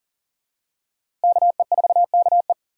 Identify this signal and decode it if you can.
Morse code, heard as a keyed tone, about 40 wpm, 700 Hz. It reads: KE4KE
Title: USA Call Signs - 5 Characters - Repeated 2x - 40wpm